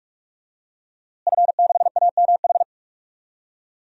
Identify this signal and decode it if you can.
Morse code, heard as a keyed tone, about 45 wpm, 700 Hz. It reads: F6AMH
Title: DXCC-100 Call Signs - 5 Characters - Repeated 2x - 45wpm